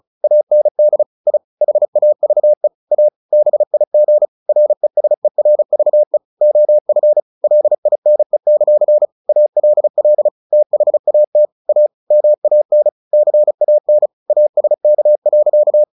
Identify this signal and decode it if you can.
Morse code, heard as a keyed tone, about 34 wpm, 600 Hz. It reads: AND I HAVE A BIG RESERVE OF LINE; ALL THAT A MAN CAN ASK.
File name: the old man and the sea / the old man and the sea 1 35wpm